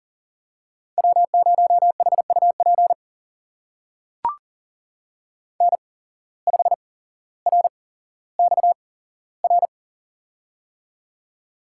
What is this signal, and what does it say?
Morse code, heard as a keyed tone, about 41 wpm, spaced as for 17 wpm, 700 Hz. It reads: W0HUP N5RXR
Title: USA Call Signs - 5 Characters - 8x Character Spacing - 40wpm